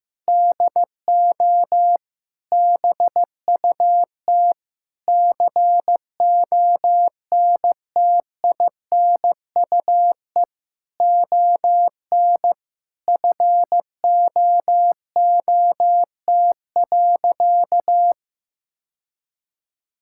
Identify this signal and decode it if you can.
Morse code, heard as a keyed tone, about 15 wpm, 700 Hz. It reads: DO BUT CONTINUE ON FOOT.
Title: Sentences from Top 1000 Words - Review Part 6 - 15wpm